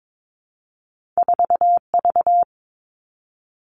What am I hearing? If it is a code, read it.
Morse code, heard as a keyed tone, about 22 wpm, 700 Hz. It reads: 4V
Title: Two Challenging Letter-Number Combinations 22wpm